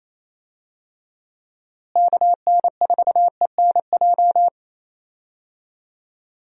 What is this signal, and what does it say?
Morse code, heard as a keyed tone, about 27 wpm, 700 Hz. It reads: KN4ENJ
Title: USA Call Signs - 6 Characters - Repeated 2x - 28wpm